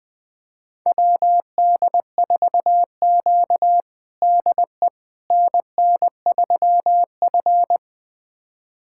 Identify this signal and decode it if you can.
Morse code, heard as a keyed tone, about 20 wpm, 700 Hz. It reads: WD4Q DE NN3F
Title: USA Callsign DE Callsign - 4 Characters - 20wpm